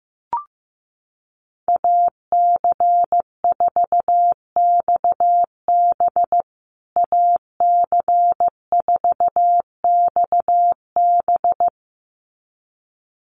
Morse code, heard as a keyed tone, about 15 wpm, 700 Hz. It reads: AC4XB AC4XB
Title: USA Call Signs Repeated 2x 15wpm